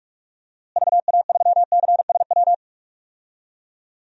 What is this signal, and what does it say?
Morse code, heard as a keyed tone, about 46 wpm, 700 Hz. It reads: VA3CSW
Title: DXCC-100 Call Signs - 6 Characters - Repeated 2x - 45wpm